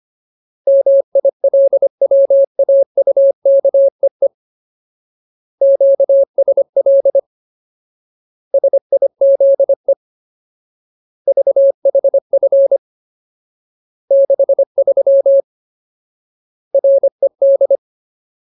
Morse code, heard as a keyed tone, about 25 wpm, 550 Hz. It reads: MILWAUKEE QSL SIZE VHF 63 RED
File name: Ultimate Word List / 25WPM Ultimate